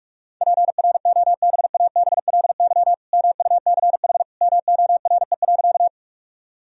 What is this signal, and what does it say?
Morse code, heard as a keyed tone, about 42 wpm, 700 Hz. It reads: PROBABLY MUCH MORE.
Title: Sentences from Top 1000 Words - Review Part 4 - 45wpm